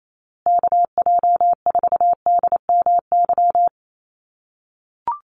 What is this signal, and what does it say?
Morse code, heard as a keyed tone, about 28 wpm, 700 Hz. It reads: KJ4DMY
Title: USA Call Signs Repeated 2x 28wpm